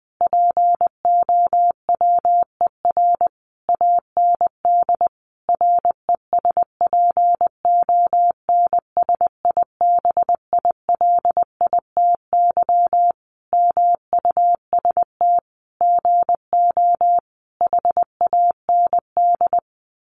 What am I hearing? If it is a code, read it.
Morse code, heard as a keyed tone, about 19 wpm, 700 Hz. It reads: POWER AND RESPONSIBILITY MUST GO HAND
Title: Sentences from Top 1200 Words 20wpm